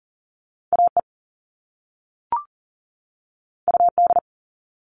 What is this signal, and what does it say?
Morse code, heard as a keyed tone, about 40 wpm, 700 Hz. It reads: AE UD
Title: All Cut Numbers - 2-Digit 40wpm